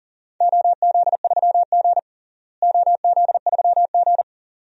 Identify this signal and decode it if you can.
Morse code, heard as a keyed tone, about 40 wpm, 700 Hz. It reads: OZ3G OZ3G
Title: DXCC-100 Call Signs - 4 Characters - Repeated 2x - 40wpm